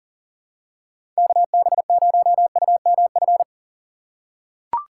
Morse code, heard as a keyed tone, about 40 wpm, 700 Hz. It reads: KB0UMF
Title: USA Call Signs - 6 Characters - 6x Character Spacing - 40wpm